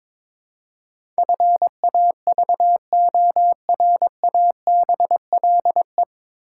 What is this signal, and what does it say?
Morse code, heard as a keyed tone, about 22 wpm, 700 Hz. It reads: FAVORABLE
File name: 9 Letter Words - First 3 Letters Given - 22wpm